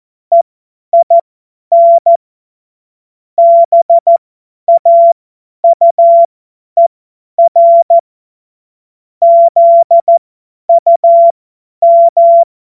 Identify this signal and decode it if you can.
Morse code, heard as a keyed tone, about 14 wpm, spaced as for 10 wpm, 680 Hz. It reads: EIN BAUER ZUM
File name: Witz-29-Juli-2025-50-F70